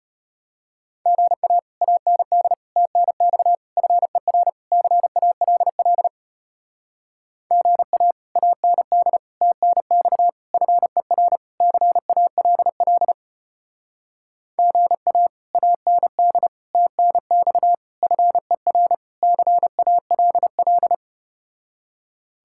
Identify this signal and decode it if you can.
Morse code, heard as a keyed tone, about 33 wpm, 700 Hz. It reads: GA AND TNX FER CALL GA AND TNX FER CALL GA AND TNX FER CALL